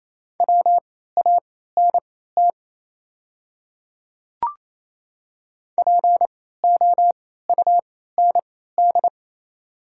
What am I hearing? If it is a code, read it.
Morse code, heard as a keyed tone, about 28 wpm, spaced as for 16 wpm, 690 Hz. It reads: WANT POUND
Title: 3x Character Spacing - Top 500 Words - 28wpm